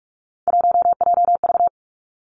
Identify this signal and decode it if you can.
Morse code, heard as a keyed tone, about 46 wpm, 700 Hz. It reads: 1JV
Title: Three Challenging Letter-Number Combinations 45wpm